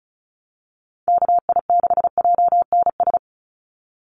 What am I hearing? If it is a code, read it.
Morse code, heard as a keyed tone, about 34 wpm, 700 Hz. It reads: KI6JNS
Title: USA Call Signs 35wpm